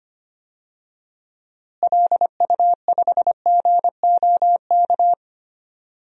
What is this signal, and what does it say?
Morse code, heard as a keyed tone, about 25 wpm, 700 Hz. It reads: LU5GOK
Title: DXCC-100 Call Signs - 6 Characters - Repeated 3x - 25wpm